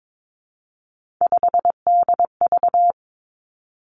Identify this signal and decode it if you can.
Morse code, heard as a keyed tone, about 22 wpm, 700 Hz. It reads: 5DV
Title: Three Challenging Letter-Number Combinations 22wpm